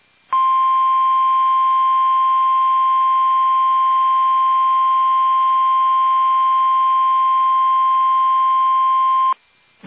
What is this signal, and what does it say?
Emergency Alert System (SAME) tones (the 1050 Hz attention tone). It is heard from 0.3 to 9.4 s.
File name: eas / KUSO / KUSO